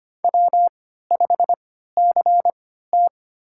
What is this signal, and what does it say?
Morse code, heard as a keyed tone, about 25 wpm, spaced as for 14 wpm, 700 Hz. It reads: W5CT
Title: USA Call Signs - 4 Characters - 3x Character Spacing - 25wpm